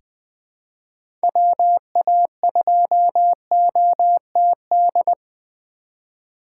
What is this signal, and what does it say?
Morse code, heard as a keyed tone, about 20 wpm, 700 Hz. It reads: WA2OTD
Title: USA Call Signs - 6 Characters - 4x Character Spacing - 20wpm